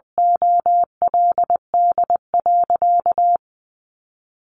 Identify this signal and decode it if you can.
Morse code, heard as a keyed tone, about 20 wpm, 700 Hz. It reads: OLD.
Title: Sentences from Top 1300 Words 20wpm